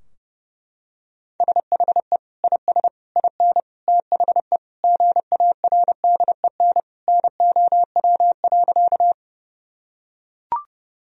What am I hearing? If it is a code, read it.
Morse code, heard as a keyed tone, about 30 wpm, 700 Hz. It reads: SHE IS IN THE GARDEN NOW.
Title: Sentences from Top 1000 Words - Review Part 4 - 30wpm